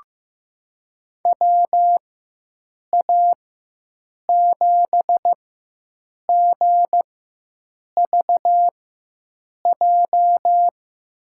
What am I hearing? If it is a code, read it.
Morse code, heard as a keyed tone, about 15 wpm, spaced as for 7 wpm, 700 Hz. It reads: WA7GVJ
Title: USA Call Signs - 6 Characters - 4x Character Spacing - 15wpm